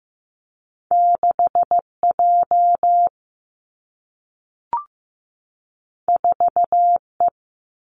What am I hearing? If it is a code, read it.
Morse code, heard as a keyed tone, about 15 wpm, 700 Hz. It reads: 6 J 4 E
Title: Two Challenging Letter-Number Combinations 15wpm